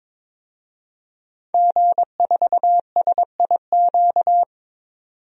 Morse code, heard as a keyed tone, about 22 wpm, 700 Hz. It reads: G4SIQ
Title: DXCC-100 Call Signs - 5 Characters - 22wpm